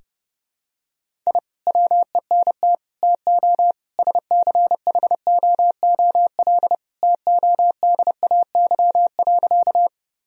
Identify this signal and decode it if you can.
Morse code, heard as a keyed tone, about 32 wpm, 700 Hz. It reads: I WENT TO SCHOOL TODAY.
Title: Sentences from Top 1000 Words - Review Part 1 - 30wpm